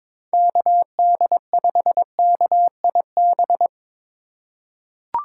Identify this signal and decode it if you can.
Morse code, heard as a keyed tone, about 22 wpm, 700 Hz. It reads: KD5KIB
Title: USA Call Signs - 6 Characters - 3x Character Spacing - 22wpm